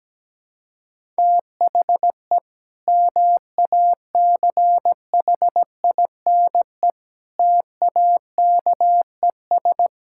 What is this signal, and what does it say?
Morse code, heard as a keyed tone, about 17 wpm, 700 Hz. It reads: THE MACHINE TAKES
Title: Sentences from Top 1000 Words - Review Part 6 - 17wpm